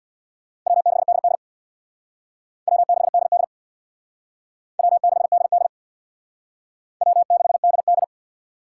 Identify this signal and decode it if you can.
Morse code, heard as a keyed tone, about 62 wpm, 700 Hz. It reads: W6DD W6DD W6DD W6DD
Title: Speed Racing - USA 1x2 Call Signs 50wpm